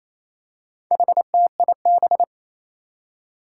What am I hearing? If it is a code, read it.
Morse code, heard as a keyed tone, about 28 wpm, 700 Hz. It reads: HTIB
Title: Four Challenging Letter-Number Combinations 28wpm